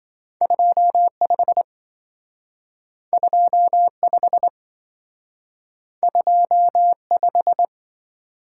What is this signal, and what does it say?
Morse code, heard as a keyed tone, about 27 wpm, 700 Hz. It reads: 25 25 25